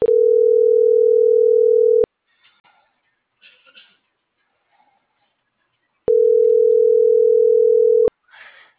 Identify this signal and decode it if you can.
Ringback tone (North America), 440 + 480 Hz.